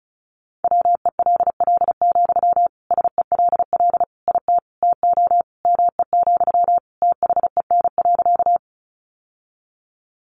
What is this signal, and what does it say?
Morse code, heard as a keyed tone, about 34 wpm, 700 Hz. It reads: WELL, SELL IT TO ME, THEN.
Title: Sentences from Top 1100 Words 35wpm